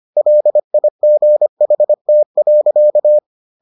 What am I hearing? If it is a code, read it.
Morse code, heard as a keyed tone, about 25 wpm, 600 Hz. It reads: LIGHT.